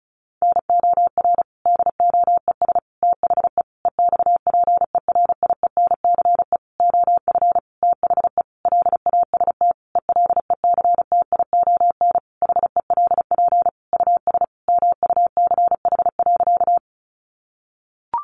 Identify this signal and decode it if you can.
Morse code, heard as a keyed tone, about 34 wpm, 700 Hz. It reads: NOR DOES THE EXPERIENCE OF THE LAST ELECTION HELP US MUCH.
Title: Sentences from Top 1500 Words 35wpm